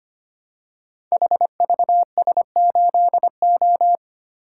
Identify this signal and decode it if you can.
Morse code, heard as a keyed tone, about 25 wpm, 700 Hz. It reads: HVS8O
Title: Code Groups - 5 Letter:Numbers - 25wpm